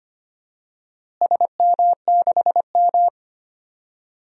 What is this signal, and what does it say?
Morse code, heard as a keyed tone, about 25 wpm, 700 Hz. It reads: SM6M
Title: DXCC-100 Call Signs - 4 Characters - Repeated 2x - 25wpm